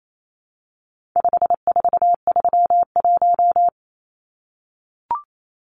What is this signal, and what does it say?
Morse code, heard as a keyed tone, about 28 wpm, 700 Hz. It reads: 5431
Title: Numbers - 4-Digit 28wpm